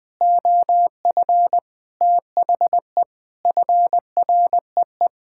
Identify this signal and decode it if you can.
Morse code, heard as a keyed tone, about 19 wpm, 700 Hz. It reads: OF THE FREE
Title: Sentences with 8-Word Phrases - 20wpm